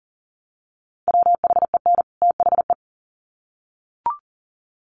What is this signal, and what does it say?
Morse code, heard as a keyed tone, about 40 wpm, 700 Hz. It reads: WHEN THE
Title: Sets of 2 Words 40wpm